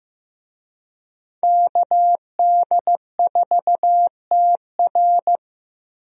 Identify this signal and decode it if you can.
Morse code, heard as a keyed tone, about 15 wpm, 700 Hz. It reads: KD4TR
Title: USA Call Signs - 5 Characters - Repeated 3x - 15wpm